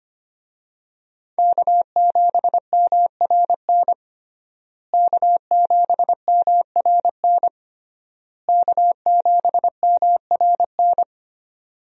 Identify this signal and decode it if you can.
Morse code, heard as a keyed tone, about 25 wpm, 700 Hz. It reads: K7MRN K7MRN K7MRN
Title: USA Call Signs - 5 Characters - Repeated 3x - 25wpm